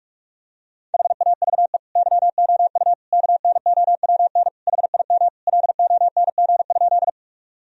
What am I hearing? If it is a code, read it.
Morse code, heard as a keyed tone, about 46 wpm, 700 Hz. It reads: HAVE YOU KNOWN HIM LONG?
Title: Sentences from Top 1000 Words - Review Part 3 - 45wpm